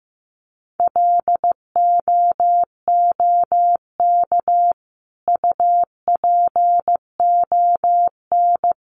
Morse code, heard as a keyed tone, about 15 wpm, 700 Hz. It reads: LOOK UPON
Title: Sentences from Top 1100 Words 15wpm